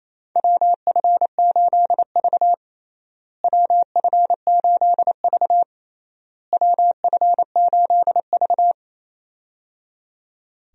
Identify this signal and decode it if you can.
Morse code, heard as a keyed tone, about 28 wpm, 700 Hz. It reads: WF8V WF8V WF8V
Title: USA Call Signs - 4 Characters - Repeated 3x - 28wpm